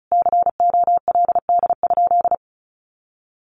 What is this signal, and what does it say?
Morse code, heard as a keyed tone, about 34 wpm, 700 Hz. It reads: COLD?
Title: Sentences from Top 1500 Words 35wpm